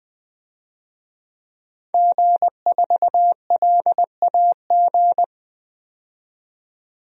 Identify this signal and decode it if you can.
Morse code, heard as a keyed tone, about 20 wpm, 700 Hz. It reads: G4LAG
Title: DXCC-100 Call Signs - 5 Characters - 20wpm